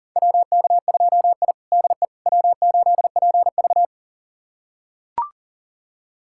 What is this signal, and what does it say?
Morse code, heard as a keyed tone, about 40 wpm, 700 Hz. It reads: WK2I DE W8PV
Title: USA Callsign DE Callsign - 4 Characters - 40wpm